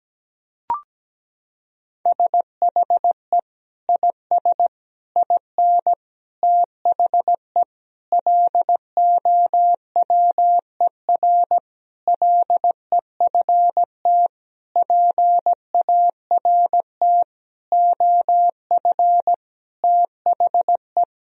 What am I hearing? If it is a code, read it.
Morse code, heard as a keyed tone, about 17 wpm, 700 Hz. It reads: SHE IS IN THE LOWER LEFT PART OF THE
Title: Sentences from Top 1000 Words - Review Part 4 - 17wpm